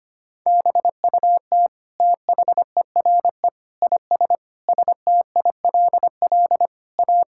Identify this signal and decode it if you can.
Morse code, heard as a keyed tone, about 25 wpm, 700 Hz. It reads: BUT THERE IS STILL A
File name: practice / 25wpm / Sentences with 9-Word Phrases - 25wpm